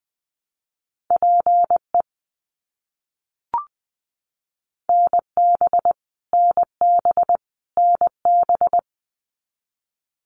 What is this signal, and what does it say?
Morse code, heard as a keyed tone, about 20 wpm, 700 Hz. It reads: PE NB NB NB
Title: Canadian Provinces and Territories - Repeated 3x - 20wpm